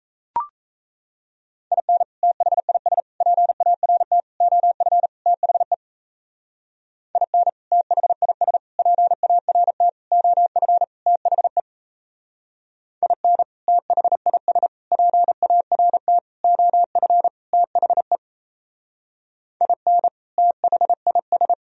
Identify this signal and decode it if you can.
Morse code, heard as a keyed tone, about 34 wpm, 700 Hz. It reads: IN THIS PART OF THE IN THIS PART OF THE IN THIS PART OF THE IN THIS
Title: Speed-Racing - Sets of 5 Words 28wpm